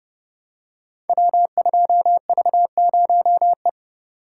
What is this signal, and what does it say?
Morse code, heard as a keyed tone, about 30 wpm, 700 Hz. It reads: W2V0E
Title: Code Groups - 5 Letter:Numbers - 30wpm